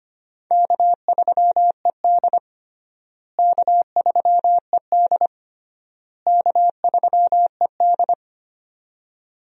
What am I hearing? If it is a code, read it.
Morse code, heard as a keyed tone, about 25 wpm, 700 Hz. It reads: K3ED K3ED K3ED
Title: USA Call Signs - 4 Characters - Repeated 3x - 25wpm